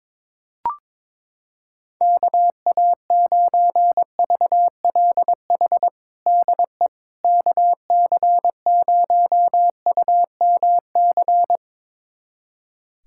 Morse code, heard as a keyed tone, about 22 wpm, 700 Hz. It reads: KA9VLH DE KC0UMC
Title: USA Callsign DE Callsign - 6 Characters - 22wpm